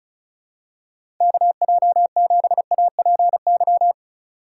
Morse code, heard as a keyed tone, about 36 wpm, 700 Hz. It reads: KJ7APY
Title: USA Call Signs - 4x Character Spacing - 35wpm